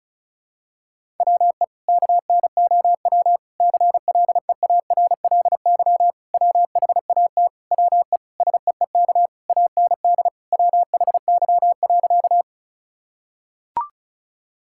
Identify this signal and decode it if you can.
Morse code, heard as a keyed tone, about 36 wpm, 700 Hz. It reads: WE KNOW CLEARLY WHAT WE SEEK AND WHY.
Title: Sentences from Top 1600 Words 35wpm